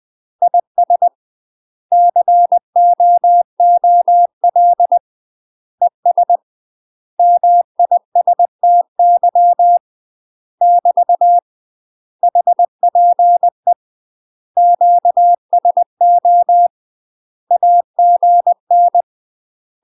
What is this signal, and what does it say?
Morse code, heard as a keyed tone, about 20 wpm, 700 Hz. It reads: IS COOL ES MISTY <BT> HPE QSO AGN
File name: UK MASTER QSOs TRAINING / 20WPM / UK MASTER QSOs 0005